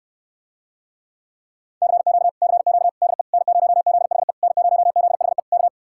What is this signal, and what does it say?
Morse code, heard as a keyed tone, about 69 wpm, 700 Hz. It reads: CQ CQ DE N0ZHE N0ZHE K